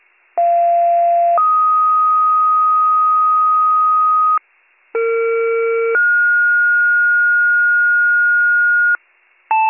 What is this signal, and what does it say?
Two-tone page, heard as a one-second tone, then a three-second tone, about 690 then 1250 Hz, about 460 then 1430 Hz, with noise. It is heard from 0.4 to 9.0 s.